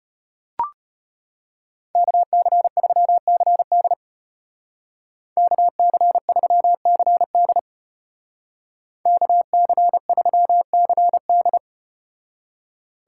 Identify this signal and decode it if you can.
Morse code, heard as a keyed tone, about 34 wpm, 700 Hz. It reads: KC3CD KC3CD KC3CD